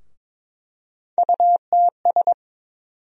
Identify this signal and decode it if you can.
Morse code, heard as a keyed tone, about 23 wpm, 700 Hz. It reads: UTS
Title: USA Call Sign Postfixes - 3 Letters 22wpm